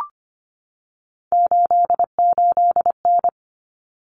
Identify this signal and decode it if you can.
Morse code, heard as a keyed tone, about 25 wpm, 700 Hz. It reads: 88N
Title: Common Cut Numbers - 3-Digit 25wpm